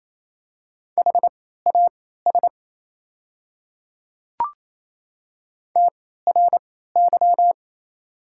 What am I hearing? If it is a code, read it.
Morse code, heard as a keyed tone, about 28 wpm, spaced as for 16 wpm, 700 Hz. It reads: HAS TRY
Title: 3x Character Spacing - Top 300 Words - 28wpm